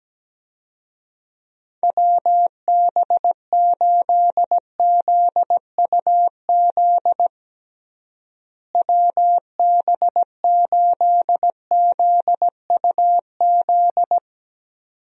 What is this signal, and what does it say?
Morse code, heard as a keyed tone, about 17 wpm, 700 Hz. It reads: WB8ZUZ WB8ZUZ
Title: USA Call Signs - 6 Characters - Repeated 2x - 17wpm